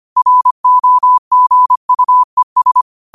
Morse code, heard as a keyed tone, about 25 wpm, 1000 Hz. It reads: ROGUES
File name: qotdfiles / 25wpm